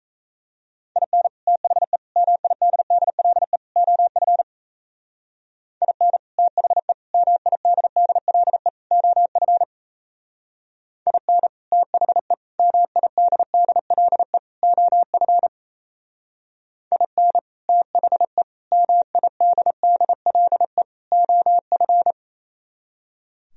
Morse code, heard as a keyed tone, about 33 wpm, 700 Hz. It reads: IN THE MIDDLE OF IN THE MIDDLE OF IN THE MIDDLE OF IN THE MIDDLE OF
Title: Speed-Racing - Sets of 4 Words 28wpm